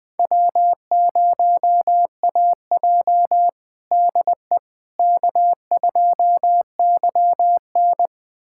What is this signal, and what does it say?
Morse code, heard as a keyed tone, about 21 wpm, 700 Hz. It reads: W0AJ DE K2YN
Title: USA Callsign DE Callsign - 4 Characters - 20wpm